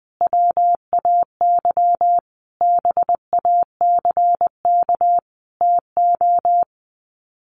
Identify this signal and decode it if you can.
Morse code, heard as a keyed tone, about 20 wpm, 700 Hz. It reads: WAY BACK TO
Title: Sets of 5 Words Encore 20wpm